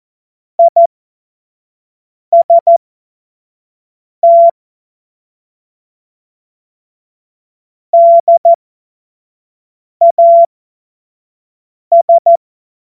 Morse code, heard as a keyed tone, about 14 wpm, spaced as for 5 wpm, 680 Hz. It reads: IST DAS
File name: Morsekurs / Witze / Witz 15. Juni 2021, 25 F70